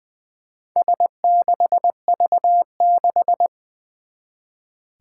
Morse code, heard as a keyed tone, about 21 wpm, 700 Hz. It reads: S6V6
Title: Four Challenging Letter-Number Combinations 20wpm